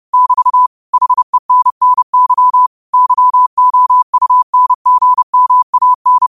Morse code, heard as a keyed tone, about 30 wpm, 1000 Hz. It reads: <BT> HENNY YOUNGMAN
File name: qotdfiles / 30wpm